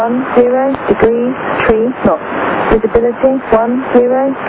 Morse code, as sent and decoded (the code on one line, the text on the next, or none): none